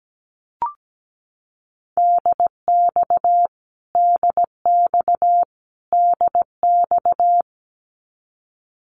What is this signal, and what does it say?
Morse code, heard as a keyed tone, about 17 wpm, 700 Hz. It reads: DX DX DX
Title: Abbreviations Repeated 3x - 17wpm